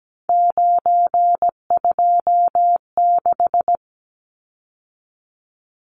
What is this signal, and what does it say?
Morse code, heard as a keyed tone, about 17 wpm, 700 Hz. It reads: 926
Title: Numbers - 3-Digit 17wpm